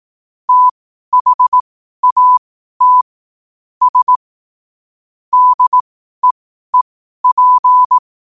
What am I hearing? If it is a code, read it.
Morse code, heard as a keyed tone, about 18 wpm, spaced as for 12 wpm, 1000 Hz. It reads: THAT S DEEP
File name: qotdfiles / 12wpm